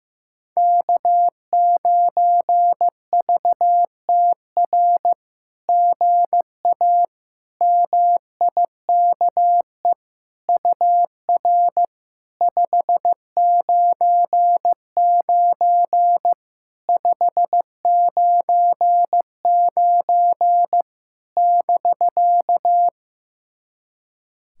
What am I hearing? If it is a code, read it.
Morse code, heard as a keyed tone, about 15 wpm, 700 Hz. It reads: K9VTR GA MIKE UR 599 599 <BK>
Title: SOTA QSO Practice 15wpm